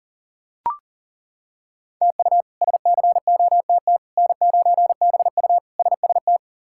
Morse code, heard as a keyed tone, about 40 wpm, 700 Hz. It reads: TU SCOTT N9BU SST
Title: K1USN SST QSO Practice - 40wpm